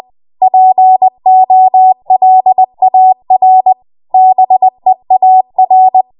Morse code, heard as a keyed tone, about 20 wpm, 750 Hz. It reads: POLAR BEAR
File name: Morse / 20WPM